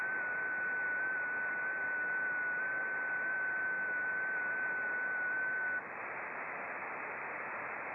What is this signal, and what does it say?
FT8.
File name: contests / sept-vhf-k5tr- -0300